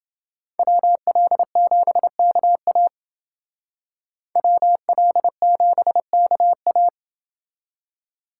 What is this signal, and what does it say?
Morse code, heard as a keyed tone, about 29 wpm, 700 Hz. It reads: WL7KA WL7KA